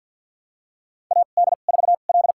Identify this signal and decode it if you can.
Morse code, heard as a keyed tone, about 52 wpm, spaced as for 37 wpm, 700 Hz. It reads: AD4L
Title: USA Call Signs - 2x Character Spacing - 50wpm